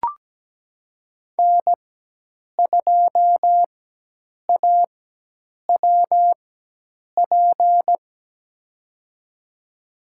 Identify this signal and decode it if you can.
Morse code, heard as a keyed tone, about 17 wpm, spaced as for 8 wpm, 700 Hz. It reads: N2AWP
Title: USA Call Signs - 5 Characters - 4x Character Spacing - 17wpm